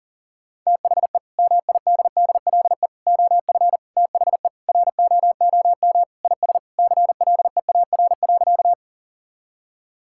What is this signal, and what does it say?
Morse code, heard as a keyed tone, about 40 wpm, 700 Hz. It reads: THE MIDDLE OF THE ROOM IS CLEAR.
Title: Sentences from Top 1000 Words - Review Part 6 - 40wpm